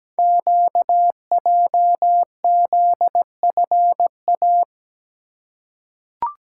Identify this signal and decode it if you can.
Morse code, heard as a keyed tone, about 17 wpm, 700 Hz. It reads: QJZFA
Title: Code Groups - 5 Letters - 17wpm